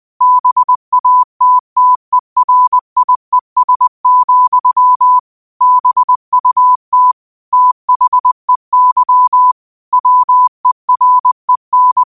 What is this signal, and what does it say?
Morse code, heard as a keyed tone, about 20 wpm, 1000 Hz. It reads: BATTERIES, BUT THEY WEREN